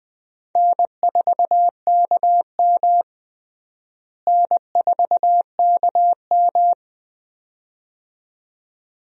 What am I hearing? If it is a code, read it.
Morse code, heard as a keyed tone, about 20 wpm, 700 Hz. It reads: N4KM N4KM
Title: USA Call Signs - 4 Characters - Repeated 2x - 20wpm